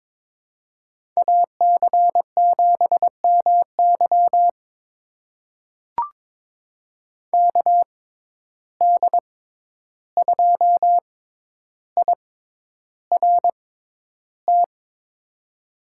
Morse code, heard as a keyed tone, about 22 wpm, spaced as for 11 wpm, 700 Hz. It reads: AC7MY KD2IRT
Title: USA Call Signs - 6x Character Spacing - 22wpm